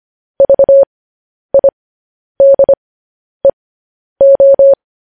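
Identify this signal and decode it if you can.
Morse code, heard as a keyed tone, about 25 wpm, spaced as for 10 wpm, 550 Hz. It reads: VIDEO
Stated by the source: Morse code news headlines practice at 10 wpm.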